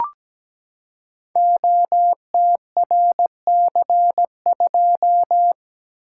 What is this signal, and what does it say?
Morse code, heard as a keyed tone, about 17 wpm, 700 Hz. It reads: OTRC2
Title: Code Groups - 5 Letter:Numbers - 17wpm